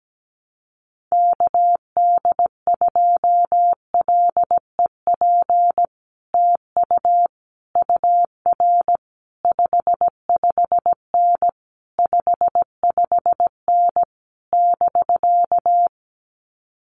Morse code, heard as a keyed tone, about 17 wpm, 700 Hz. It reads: KD2LEP TU UR 55N 55N <BK>
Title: Common QSO Phrases Encore 17wpm